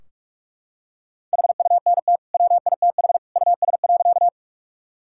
Morse code, heard as a keyed tone, about 46 wpm, 700 Hz. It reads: HUNT WITH US.
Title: Sentences from Top 1600 Words 45wpm